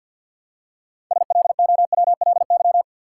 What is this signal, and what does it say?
Morse code, heard as a keyed tone, about 53 wpm, 700 Hz. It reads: SLOWLY
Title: Six Letter Words - 50wpm